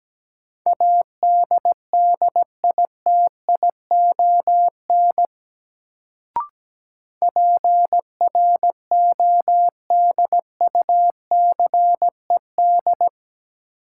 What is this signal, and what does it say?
Morse code, heard as a keyed tone, about 17 wpm, 700 Hz. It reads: ADDITION PRODUCED
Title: Sentences from Top 1000 Words - Review Part 6 - New Word Drill - 17wpm